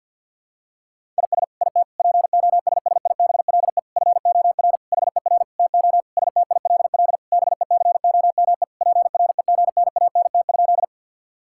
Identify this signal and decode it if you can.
Morse code, heard as a keyed tone, about 50 wpm, 700 Hz. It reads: IS IT POSSIBLE FOR HER TO STILL BECOME PREGNANT?
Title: Sentences from Top 1000 Words - Review Part 6 - 50wpm